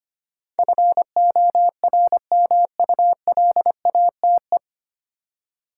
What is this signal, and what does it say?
Morse code, heard as a keyed tone, about 25 wpm, 700 Hz. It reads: FORMULATE
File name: Nine Letter Words - 25wpm